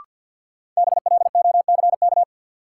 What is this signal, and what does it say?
Morse code, heard as a keyed tone, about 52 wpm, 700 Hz. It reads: BLOCK
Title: Sentences from Top 1600 Words - New Word Drill - 50wpm